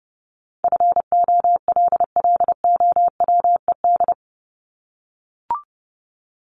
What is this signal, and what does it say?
Morse code, heard as a keyed tone, about 30 wpm, 700 Hz. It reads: FOLLOWED
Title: ED Suffix - Top 100 Words with Base Words 30wpm